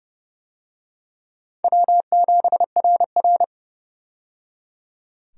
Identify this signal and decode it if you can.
Morse code, heard as a keyed tone, about 29 wpm, 700 Hz. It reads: W7RR
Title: USA Call Signs - 4 Characters - 30wpm